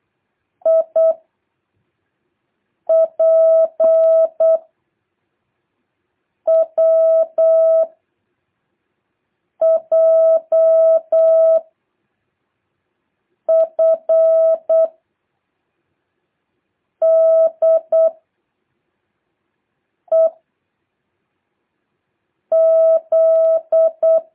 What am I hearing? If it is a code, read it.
Morse code, heard as a keyed tone, about 8 wpm, 650 Hz. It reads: I P W J F D E Z
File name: jh8cbh / 2023cw